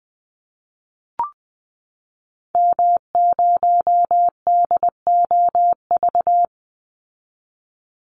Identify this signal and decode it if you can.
Morse code, heard as a keyed tone, about 19 wpm, 700 Hz. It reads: M0DOV
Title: Call Signs Repeated 2x - 20wpm